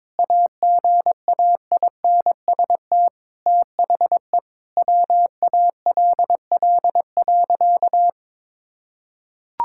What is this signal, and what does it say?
Morse code, heard as a keyed tone, about 22 wpm, 700 Hz. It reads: AGAINST THE WALL.
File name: practice / 22wpm / Sentences with 2-Word Phrases - 22wpm